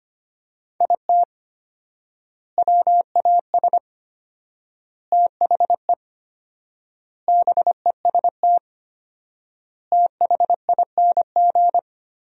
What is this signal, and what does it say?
Morse code, heard as a keyed tone, about 25 wpm, 700 Hz. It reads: IT WAS THE BEST THING
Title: 4x Word-Spacing - Sentences from Top 400 Words - 25wpm